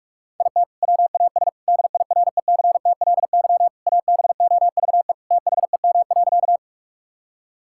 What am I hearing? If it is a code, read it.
Morse code, heard as a keyed tone, about 41 wpm, 700 Hz. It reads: IT WAS DIRECTLY ABOVE THEM.
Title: Sentences from Top 1000 Words - Review Part 5 - 45wpm